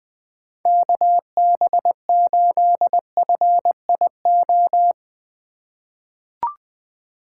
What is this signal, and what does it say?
Morse code, heard as a keyed tone, about 20 wpm, 700 Hz. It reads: KB8FIO
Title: USA Call Signs - 6 Characters - 3x Character Spacing - 20wpm